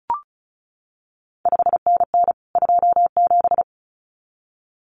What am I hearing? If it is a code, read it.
Morse code, heard as a keyed tone, about 36 wpm, 700 Hz. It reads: 5NN 27
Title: CQ WW DX Contest Exchange Encore - 35wpm